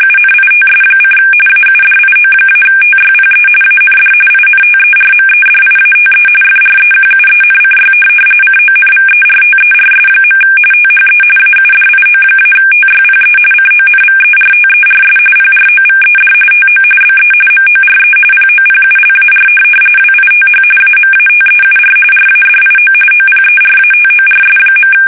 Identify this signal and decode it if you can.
RTTY, about 45.45 baud. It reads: UNITS: AUDIO-TYPE AND INTERMEDIATE FREQUENCY CONVERTERS. THE AUDIO-TYPE CONVERTERS PROVED TO BE MORE POPULAR WITH AMATEUR RADIO OPERATORS.
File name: fsk850 / teletype-RTTY